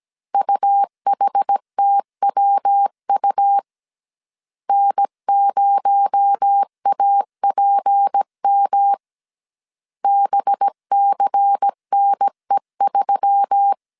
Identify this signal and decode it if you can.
Morse code, heard as a keyed tone, about 17 wpm, 800 Hz. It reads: UHTWU N0APM BCNE3
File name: CW / DET / 1 6 1500C S A 16wpm 2635